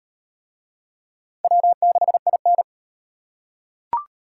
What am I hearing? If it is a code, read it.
Morse code, heard as a keyed tone, about 36 wpm, 700 Hz. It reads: W6IN